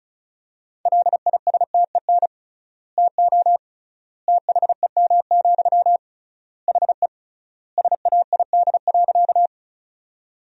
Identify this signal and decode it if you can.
Morse code, heard as a keyed tone, about 34 wpm, 700 Hz. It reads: LISTEN TO THEM, HE SAID.
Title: 3x Word-Spacing - Sentences from Top 400 Words - 35wpm